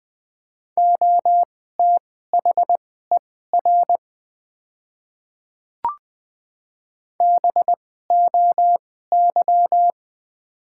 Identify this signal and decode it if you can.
Morse code, heard as a keyed tone, about 20 wpm, spaced as for 14 wpm, 700 Hz. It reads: OTHER BOY
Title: 2x Character Spacing - Top 200 Words - 20wpm